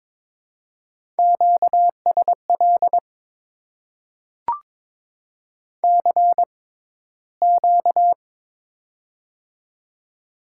Morse code, heard as a keyed tone, about 22 wpm, spaced as for 14 wpm, 700 Hz. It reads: QSL CQ
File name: Top 90 QSO Elements - 6x Character Spacing - 22wpm